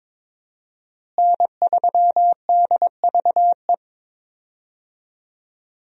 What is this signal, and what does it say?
Morse code, heard as a keyed tone, about 22 wpm, 700 Hz. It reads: N3DVE
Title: USA Call Signs - 5 Characters - Repeated 3x - 22wpm